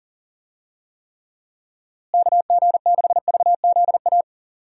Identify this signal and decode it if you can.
Morse code, heard as a keyed tone, about 40 wpm, 700 Hz. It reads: KG6VZA
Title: USA Call Signs - 6 Characters - Repeated 3x - 40wpm